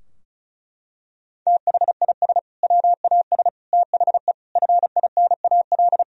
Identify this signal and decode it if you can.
Morse code, heard as a keyed tone, about 36 wpm, 700 Hz. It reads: THIS WAS THE FINAL
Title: Sentences from Top 1000 Words - Review Part 2 - 35wpm